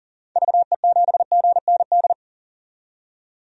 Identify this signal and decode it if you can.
Morse code, heard as a keyed tone, about 40 wpm, 700 Hz. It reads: VE7GND
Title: DXCC-100 Call Signs - 6 Characters - 40wpm